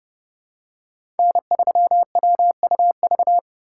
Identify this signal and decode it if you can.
Morse code, heard as a keyed tone, about 30 wpm, 700 Hz. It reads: N3WUV
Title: USA Call Signs - 5 Characters - 2x Character Spacing - 30wpm